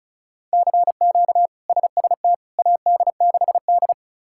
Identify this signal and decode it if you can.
Morse code, heard as a keyed tone, about 36 wpm, 700 Hz. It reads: CQ SST AD6D
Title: K1USN SST QSO Practice - 35wpm